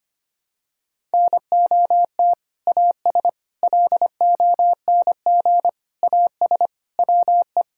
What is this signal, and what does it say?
Morse code, heard as a keyed tone, about 25 wpm, 700 Hz. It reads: NOT AS LONG AS WE
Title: Sentences from Top 1000 Words - Review Part 6 - 25wpm